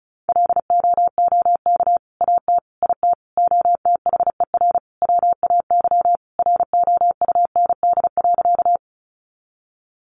Morse code, heard as a keyed tone, about 34 wpm, 700 Hz. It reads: LOOK AT IT OTHER WAY ROUND.
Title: Sentences from Top 200 Words 35wpm